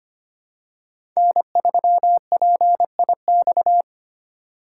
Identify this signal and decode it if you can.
Morse code, heard as a keyed tone, about 25 wpm, 700 Hz. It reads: N3PIX
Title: USA Call Signs - 5 Characters - 4x Character Spacing - 25wpm